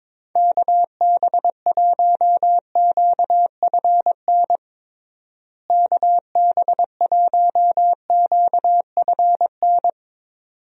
Morse code, heard as a keyed tone, about 22 wpm, 700 Hz. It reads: KB1QFN KB1QFN
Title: USA Call Signs - 6 Characters - Repeated 2x - 22wpm